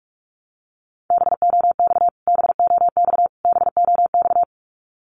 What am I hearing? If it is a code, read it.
Morse code, heard as a keyed tone, about 46 wpm, 700 Hz. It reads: BOX BOX BOX
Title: Top 500 Words Repeated 45wpm